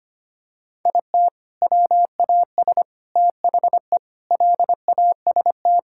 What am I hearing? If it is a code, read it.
Morse code, heard as a keyed tone, about 25 wpm, 700 Hz. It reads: IT WAS THE LAST
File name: practice / 25wpm / Sentences with 3-Word Phrases - 25wpm